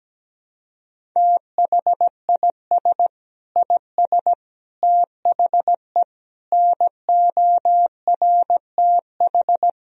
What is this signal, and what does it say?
Morse code, heard as a keyed tone, about 17 wpm, 700 Hz. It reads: THIS IS THE NORTH
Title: Sentences from Top 1000 Words - Review Part 1 - 17wpm